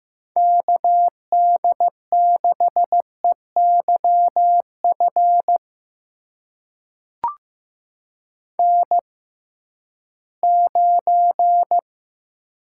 Morse code, heard as a keyed tone, about 15 wpm, spaced as for 11 wpm, 700 Hz. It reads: KD6EYF N9
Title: USA Call Signs - 6x Character Spacing - 15wpm